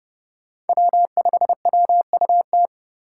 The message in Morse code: W5WUT

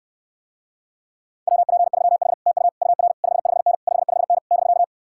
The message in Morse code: WP3H TU UR 55N 55N <BK>

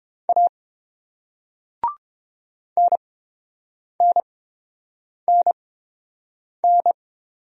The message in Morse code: A NNNN

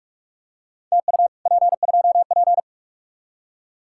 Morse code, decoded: TU P2P

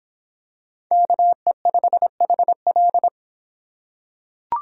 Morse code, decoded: KE5HL

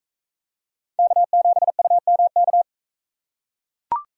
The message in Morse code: K7UMK

